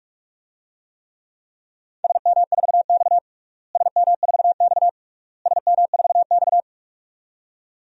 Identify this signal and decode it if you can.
Morse code, heard as a keyed tone, about 42 wpm, 700 Hz. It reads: SM4X SM4X SM4X
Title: DXCC-100 Call Signs - 4 Characters - Repeated 3x - 45wpm